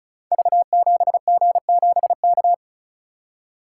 Morse code, heard as a keyed tone, about 36 wpm, 700 Hz. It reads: V7G7K
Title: Code Groups - 5 Letter:Numbers - 35wpm